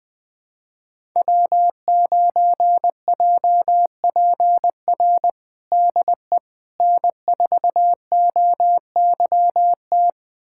Morse code, heard as a keyed tone, about 20 wpm, 700 Hz. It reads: W9JPR DE N4OYT
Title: USA Callsign DE Callsign - 5 Characters - 20wpm